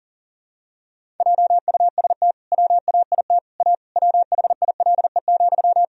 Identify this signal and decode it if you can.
Morse code, heard as a keyed tone, about 37 wpm, 700 Hz. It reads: JUST WAIT A WHILE,
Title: Sentences from Top 1000 Words - Review Part 1 - 40wpm